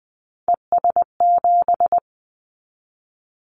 Morse code, heard as a keyed tone, about 20 wpm, 700 Hz. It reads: ES7
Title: Callsign Prefixes for Top 50 DXCC - 20wpm